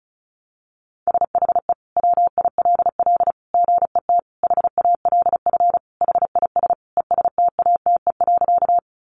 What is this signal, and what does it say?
Morse code, heard as a keyed tone, about 34 wpm, 700 Hz. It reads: SHE WILL GET HALF HIS ESTATE.
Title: Sentences from Top 1200 Words 35wpm